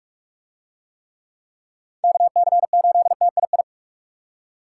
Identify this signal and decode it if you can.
Morse code, heard as a keyed tone, about 40 wpm, 700 Hz. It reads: KC8TII